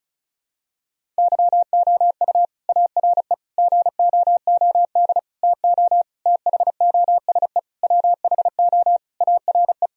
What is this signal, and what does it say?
Morse code, heard as a keyed tone, about 36 wpm, 700 Hz. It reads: YOU ARE GOOD TO THOSE WHO ARE
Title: Sentences from Top 1000 Words - Review Part 1 - 35wpm